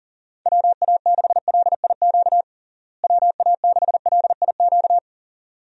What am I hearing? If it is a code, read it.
Morse code, heard as a keyed tone, about 40 wpm, 700 Hz. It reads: WA6LIQ WA6LIQ
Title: USA Call Signs - 6 Characters - Repeated 2x - 40wpm